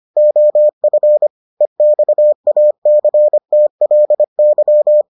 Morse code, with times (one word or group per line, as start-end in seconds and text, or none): 0.2-1.3 OF
1.6-5.0 EXACTLY